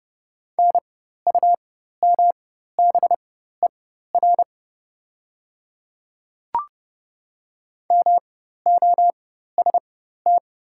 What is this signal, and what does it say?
Morse code, heard as a keyed tone, about 30 wpm, spaced as for 14 wpm, 690 Hz. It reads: NUMBER MOST
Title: 4x Character Spacing - Top 200 Words - 30wpm